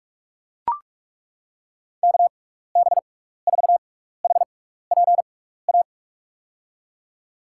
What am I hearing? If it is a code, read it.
Morse code, heard as a keyed tone, about 42 wpm, spaced as for 15 wpm, 700 Hz. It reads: KB4HPA
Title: USA Call Signs - 6x Character Spacing - 45wpm